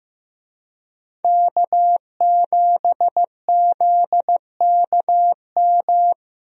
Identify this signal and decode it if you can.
Morse code, heard as a keyed tone, about 15 wpm, 700 Hz. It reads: K7ZKM
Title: USA Call Signs - 5 Characters - Repeated 2x - 15wpm